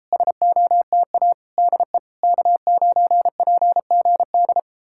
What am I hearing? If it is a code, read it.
Morse code, heard as a keyed tone, about 38 wpm, 700 Hz. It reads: SOTA DE K9PGD